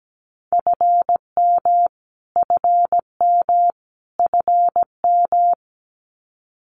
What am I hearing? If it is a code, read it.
Morse code, heard as a keyed tone, about 17 wpm, 700 Hz. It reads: FM FM FM
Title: Abbreviations Repeated 3x - 17wpm